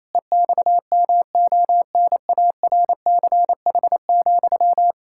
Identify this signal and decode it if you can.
Morse code, heard as a keyed tone, about 28 wpm, 700 Hz. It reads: EXMONARCH,
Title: Pangram Challenge v2 - 28wpm